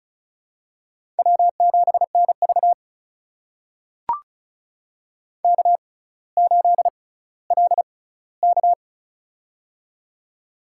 Morse code, heard as a keyed tone, about 36 wpm, spaced as for 18 wpm, 700 Hz. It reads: W7NV K8LK
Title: USA Call Signs - 6x Character Spacing - 35wpm